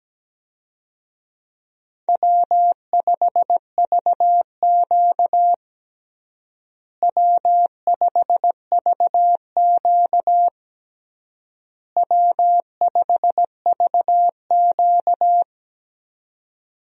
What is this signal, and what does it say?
Morse code, heard as a keyed tone, about 17 wpm, 700 Hz. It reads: W5VQ W5VQ W5VQ
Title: USA Call Signs - 4 Characters - Repeated 3x - 17wpm